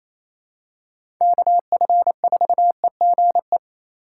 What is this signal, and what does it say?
Morse code, heard as a keyed tone, about 28 wpm, 700 Hz. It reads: KF4EGE
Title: USA Call Signs - 6 Characters - 4x Character Spacing - 28wpm